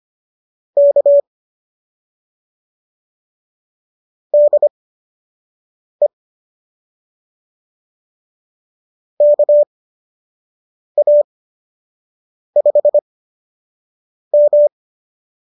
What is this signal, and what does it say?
Morse code, heard as a keyed tone, about 25 wpm, spaced as for 6 wpm, 590 Hz. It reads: K DE KA5M